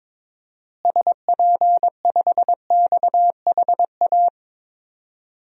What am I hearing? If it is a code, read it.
Morse code, heard as a keyed tone, about 22 wpm, 700 Hz. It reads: SP5XHA
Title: DXCC-100 Call Signs - 6 Characters - Repeated 2x - 22wpm